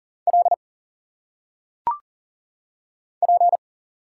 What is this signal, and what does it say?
Morse code, heard as a keyed tone, about 40 wpm, 700 Hz. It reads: L P
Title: Single Letters - Phonetic Pronunciation 40wpm